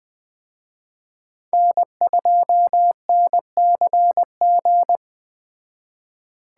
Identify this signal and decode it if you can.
Morse code, heard as a keyed tone, about 19 wpm, 700 Hz. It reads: N2NCG